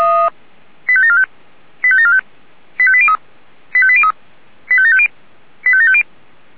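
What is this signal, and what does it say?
Five-tone selective call (ZVEI), heard as quick runs of five tones, tones of about 70 ms. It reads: 85745 85745 85803 85803 85750 85750
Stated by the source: ZVEI five-tone selective call 85726 received by SDR.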